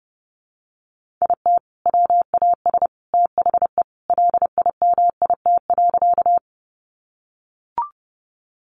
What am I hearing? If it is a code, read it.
Morse code, heard as a keyed tone, about 30 wpm, 700 Hz. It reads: IT WAS THE LIMIT.
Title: Sentences from Top 1500 Words 30wpm